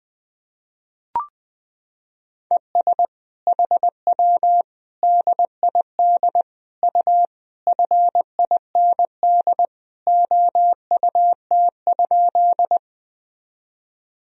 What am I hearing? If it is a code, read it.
Morse code, heard as a keyed tone, about 20 wpm, 700 Hz. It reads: ES HW DID U FIND OUT?
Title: Sentences from Top 200 Words and Abbreviations 20wpm